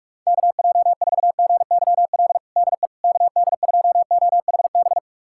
Morse code, heard as a keyed tone, about 46 wpm, 690 Hz. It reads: KJ4GYL DE KD2OHB
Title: USA Callsign DE Callsign - 6 Characters - 45wpm